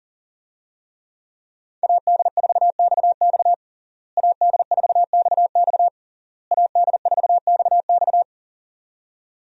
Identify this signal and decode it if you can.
Morse code, heard as a keyed tone, about 38 wpm, 700 Hz. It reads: AD4XX AD4XX AD4XX
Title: USA Call Signs - 5 Characters - Repeated 3x - 40wpm